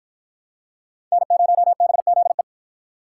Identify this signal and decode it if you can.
Morse code, heard as a keyed tone, about 50 wpm, 700 Hz. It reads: N0BZE